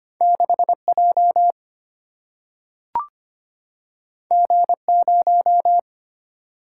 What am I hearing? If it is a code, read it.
Morse code, heard as a keyed tone, about 25 wpm, 700 Hz. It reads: 6J G0
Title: Single-Letter Single-Number 25wpm